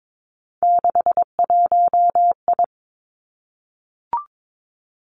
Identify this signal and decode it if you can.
Morse code, heard as a keyed tone, about 22 wpm, 700 Hz. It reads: 61I
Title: Three Challenging Letter-Number Combinations 22wpm